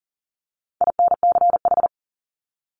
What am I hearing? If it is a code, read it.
Morse code, heard as a keyed tone, about 40 wpm, 700 Hz. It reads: INCH
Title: Top 500 Words 40wpm